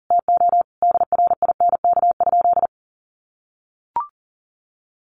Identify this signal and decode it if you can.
Morse code, heard as a keyed tone, about 40 wpm, 700 Hz. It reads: TO DRINK?
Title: Sentences from Top 1500 Words 40wpm